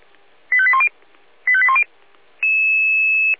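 Five-tone selective call (ZVEI), heard as quick runs of five tones, tones of about 70 ms. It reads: 85810 85810
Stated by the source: Five-tone ZVEI selective call 85857 received by SDR.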